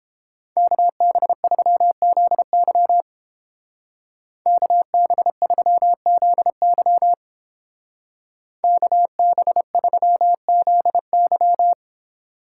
Morse code, heard as a keyed tone, about 30 wpm, 700 Hz. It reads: KB3ZY KB3ZY KB3ZY